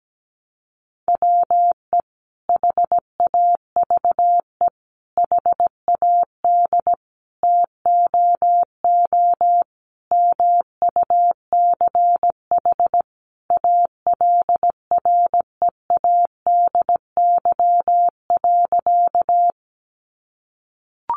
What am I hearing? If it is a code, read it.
Morse code, heard as a keyed tone, about 17 wpm, 700 Hz. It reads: WE HAVE HAD TOO MUCH ALREADY.
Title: Sentences from Top 700 Words 17wpm